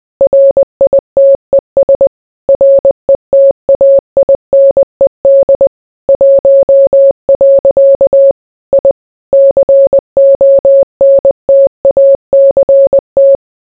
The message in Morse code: RITES RETAINED 1. I CONTACT

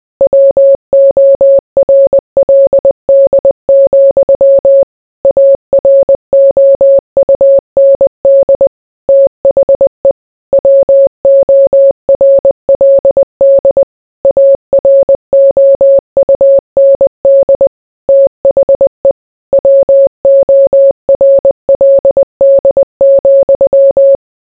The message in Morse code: WORLD, AROUND THE WORLD AROUND THE WORLD,

